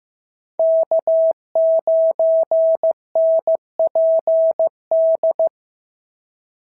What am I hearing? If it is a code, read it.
Morse code, heard as a keyed tone, about 14 wpm, 650 Hz. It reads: K9NPD
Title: SOTA QSO Practice 15wpm